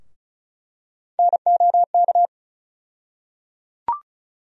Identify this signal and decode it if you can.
Morse code, heard as a keyed tone, about 36 wpm, 700 Hz. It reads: NOK
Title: USA Call Sign Postfixes - 3 Letters 35wpm